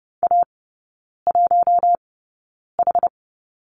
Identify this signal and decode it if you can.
Morse code, heard as a keyed tone, about 30 wpm, spaced as for 9 wpm, 700 Hz. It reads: A1H
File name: 081 Lesson 14e - Number 2 - Three Characters 30wpm